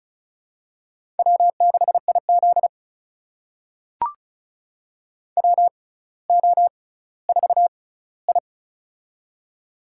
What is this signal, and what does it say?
Morse code, heard as a keyed tone, about 36 wpm, spaced as for 18 wpm, 700 Hz. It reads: W6IZ WO4I
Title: USA Call Signs - 4 Characters - 6x Character Spacing - 35wpm